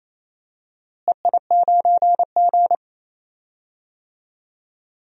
Morse code, heard as a keyed tone, about 28 wpm, 700 Hz. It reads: EI9G
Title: DXCC-100 Call Signs - 4 Characters - Repeated 2x - 28wpm